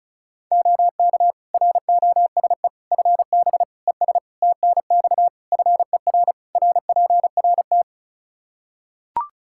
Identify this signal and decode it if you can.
Morse code, heard as a keyed tone, about 36 wpm, 700 Hz. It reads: OK ROSE FB ES TNX FER RPRT
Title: Practice QSO - Exchange 2 - Info - 35wpm